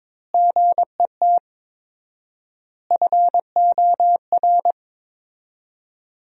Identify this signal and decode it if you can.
Morse code, heard as a keyed tone, about 22 wpm, 700 Hz. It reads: GET FOR
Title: 4x Word-Spacing - Sentences from Top 400 Words - 22wpm